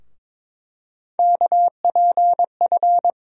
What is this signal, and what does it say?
Morse code, heard as a keyed tone, about 22 wpm, 700 Hz. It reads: KPF
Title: USA Call Sign Postfixes - 3 Letters 22wpm